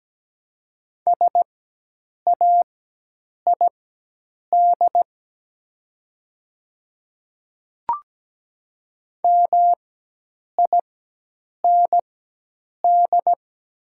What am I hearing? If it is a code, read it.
Morse code, heard as a keyed tone, about 17 wpm, spaced as for 8 wpm, 700 Hz. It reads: SAID MIND
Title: 4x Character Spacing - Top 500 Words - 17wpm